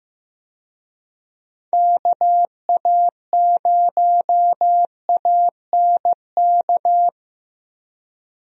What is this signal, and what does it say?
Morse code, heard as a keyed tone, about 16 wpm, 700 Hz. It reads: KA0ANK
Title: USA Call Signs - 6 Characters - Repeated 3x - 15wpm